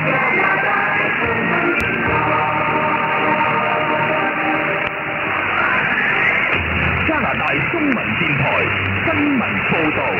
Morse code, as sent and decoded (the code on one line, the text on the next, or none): none